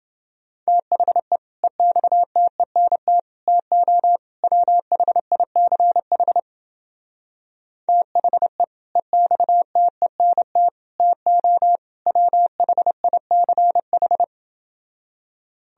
Morse code, heard as a keyed tone, about 29 wpm, 700 Hz. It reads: THE EXTENT TO WHICH THE EXTENT TO WHICH